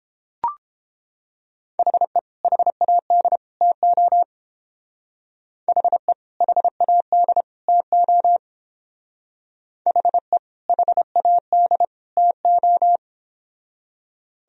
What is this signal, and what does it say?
Morse code, heard as a keyed tone, about 28 wpm, 700 Hz. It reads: HE HAD TO HE HAD TO HE HAD TO